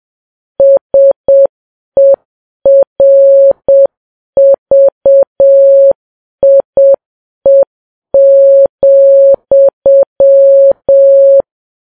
Morse code, heard as a keyed tone, about 7 wpm, 550 Hz. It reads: SERVIE,